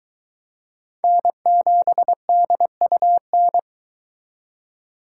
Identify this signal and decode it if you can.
Morse code, heard as a keyed tone, about 22 wpm, 700 Hz. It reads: N7DUN